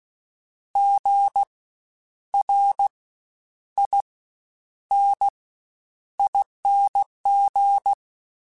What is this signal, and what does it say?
Morse code, heard as a keyed tone, about 16 wpm, spaced as for 9 wpm, 800 Hz. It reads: GRINING
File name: Spaced Alphas / Spaced Alphas mp3s / Gs sp @15